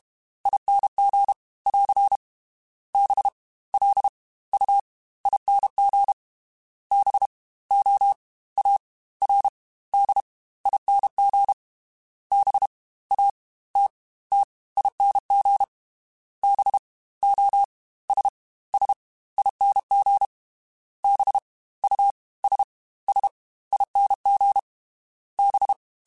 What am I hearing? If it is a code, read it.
Morse code, heard as a keyed tone, about 32 wpm, 800 Hz. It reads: ING <AR> B L U ING B O A R D ING B A T T ING B O S S ING B U S S ING B
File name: Special INGs / Special INGs-mp3s / Fast ing @30